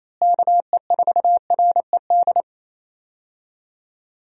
Morse code, heard as a keyed tone, about 27 wpm, 700 Hz. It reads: KE4RED
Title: USA Call Signs - 6 Characters - Repeated 3x - 28wpm